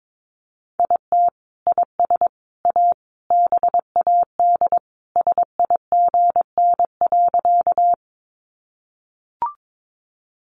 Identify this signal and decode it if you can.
Morse code, heard as a keyed tone, about 22 wpm, 700 Hz. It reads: IT IS A BAD SIGN.
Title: Sentences from Top 1000 Words 22wpm